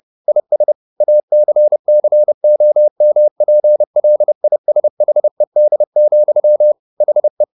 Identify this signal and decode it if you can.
Morse code, heard as a keyed tone, about 30 wpm, 600 Hz. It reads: IS ACCOMPLISHED, HE